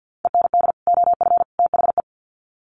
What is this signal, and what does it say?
Morse code, heard as a keyed tone, about 50 wpm, 700 Hz. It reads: END OF THE
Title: Sets of 3 Words 50wpm